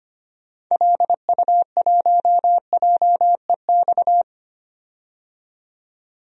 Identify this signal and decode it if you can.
Morse code, heard as a keyed tone, about 25 wpm, 700 Hz. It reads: LU1JEX
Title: DXCC-100 Call Signs - 6 Characters - 25wpm